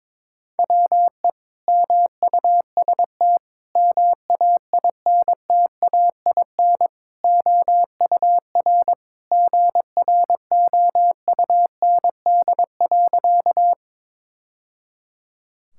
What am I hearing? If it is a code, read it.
Morse code, heard as a keyed tone, about 22 wpm, 700 Hz. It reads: WE MUST MAINTAIN OUR GROUND.
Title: Sentences from Top 1000 Words - Review Part 5 - 22wpm